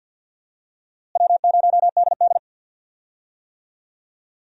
Morse code, heard as a keyed tone, about 51 wpm, 700 Hz. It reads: W0DD
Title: USA Call Signs - 4 Characters - Repeated 2x - 50wpm